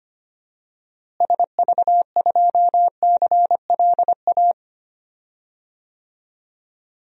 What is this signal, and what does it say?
Morse code, heard as a keyed tone, about 25 wpm, 700 Hz. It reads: SV2CLA
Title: DXCC-100 Call Signs - 6 Characters - Repeated 2x - 25wpm